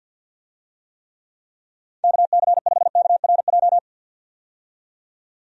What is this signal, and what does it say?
Morse code, heard as a keyed tone, about 47 wpm, 700 Hz. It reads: KC5KRJ
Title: USA Call Signs - 6 Characters - 50wpm